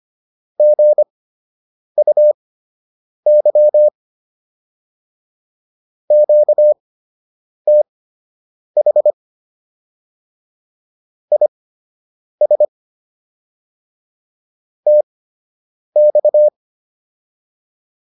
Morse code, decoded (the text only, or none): GUY QTH IS TX